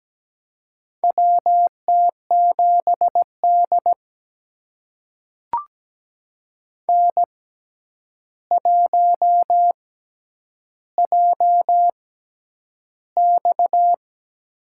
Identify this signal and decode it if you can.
Morse code, heard as a keyed tone, about 17 wpm, spaced as for 9 wpm, 700 Hz. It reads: WT7D N1JX
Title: USA Call Signs - 4 Characters - 6x Character Spacing - 17wpm